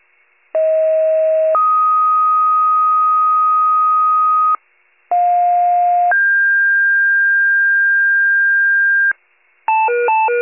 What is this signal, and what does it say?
Two-tone page, heard as a one-second tone, then a three-second tone, about 640 then 1250 Hz, about 710 then 1600 Hz, with noise. It is heard from 0.5 to 9.1 s.
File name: wcalert